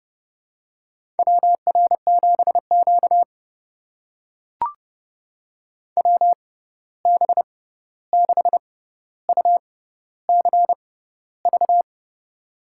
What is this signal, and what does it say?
Morse code, heard as a keyed tone, about 32 wpm, spaced as for 14 wpm, 700 Hz. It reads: WR7Q WB6UCV
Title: USA Call Signs - 6x Character Spacing - 30wpm